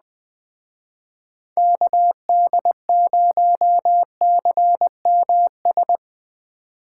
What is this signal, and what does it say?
Morse code, heard as a keyed tone, about 21 wpm, 700 Hz. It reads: KD0CMS
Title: Speed Racing - USA Call Signs 17wpm